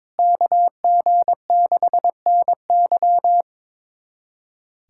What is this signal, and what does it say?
Morse code, heard as a keyed tone, about 22 wpm, 700 Hz. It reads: KG6NY
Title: USA Callsign DE Callsign - 5 Characters - 22wpm